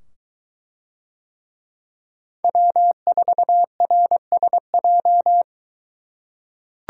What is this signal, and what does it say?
Morse code, heard as a keyed tone, about 22 wpm, 700 Hz. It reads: W4RSJ